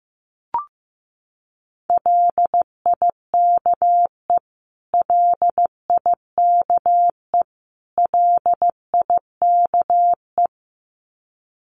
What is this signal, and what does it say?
Morse code, heard as a keyed tone, about 15 wpm, 700 Hz. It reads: LIKE LIKE LIKE
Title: Top 200 Words Repeated 15wpm